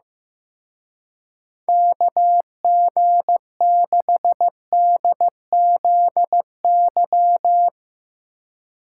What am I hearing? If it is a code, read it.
Morse code, heard as a keyed tone, about 15 wpm, 700 Hz. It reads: KG6DZY
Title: Speed Racing - USA 2x3 Call Signs 15wpm